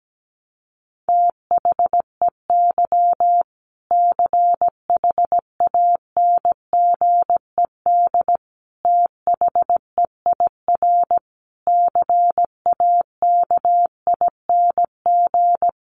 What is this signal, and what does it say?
Morse code, heard as a keyed tone, about 17 wpm, 700 Hz. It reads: THEY CHANGED THEIR CAKING
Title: A Sound Of Thunder 17wpm